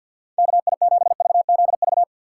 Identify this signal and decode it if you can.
Morse code, heard as a keyed tone, about 50 wpm, 700 Hz. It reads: KI7VZV
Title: USA Call Signs - 2x Character Spacing - 50wpm